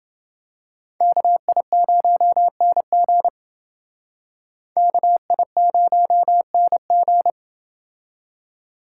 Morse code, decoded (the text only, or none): KI0NG KI0NG